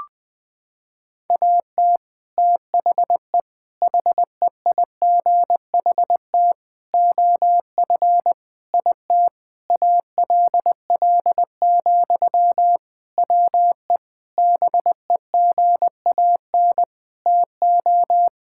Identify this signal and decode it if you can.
Morse code, heard as a keyed tone, about 21 wpm, 700 Hz. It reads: AT THE HEIGHT OF IT ALL, WE BEGAN TO
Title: Sentences from Top 1600 Words 20wpm